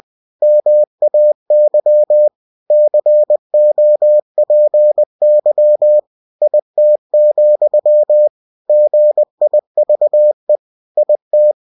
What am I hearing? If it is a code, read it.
Morse code, heard as a keyed tone, about 20 wpm, 600 Hz. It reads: MAY COPY IT, GIVE IT
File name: cw-books / winnie-the-pooh / winnie-20wpm-page-0115